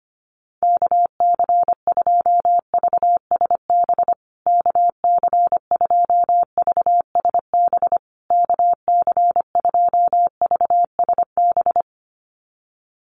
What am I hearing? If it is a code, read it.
Morse code, heard as a keyed tone, about 25 wpm, 700 Hz. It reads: KC2VSB KC2VSB KC2VSB
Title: USA Call Signs Repeated 3x 25wpm